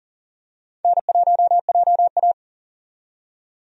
Morse code, heard as a keyed tone, about 40 wpm, 700 Hz. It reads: N1JA
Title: USA Call Signs - 4 Characters - 40wpm